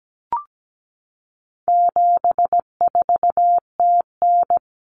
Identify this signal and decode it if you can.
Morse code, heard as a keyed tone, about 17 wpm, 700 Hz. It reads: 74TN
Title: Common Cut Numbers - 4-Digit 17wpm